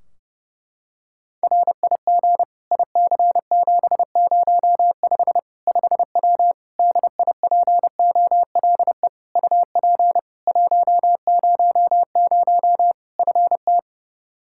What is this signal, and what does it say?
Morse code, heard as a keyed tone, about 30 wpm, 700 Hz. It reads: RIG IC705 5W DIPOLE UP 100 FT
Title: K1USN SST QSO Practice 30wpm